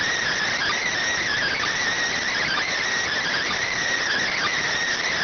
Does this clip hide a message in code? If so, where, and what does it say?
Yes: Emergency Alert System (SAME) tones from 0.5 to 4.6 s, heard as harsh buzzing bursts.